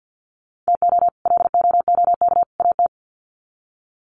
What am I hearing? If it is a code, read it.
Morse code, heard as a keyed tone, about 50 wpm, 700 Hz. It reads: TO LOOK AT
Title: Sets of 3 Words 50wpm